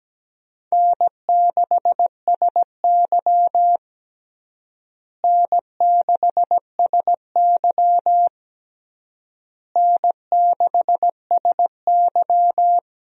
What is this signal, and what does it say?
Morse code, heard as a keyed tone, about 17 wpm, 700 Hz. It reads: N6SY N6SY N6SY
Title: USA Call Signs - 4 Characters - Repeated 3x - 17wpm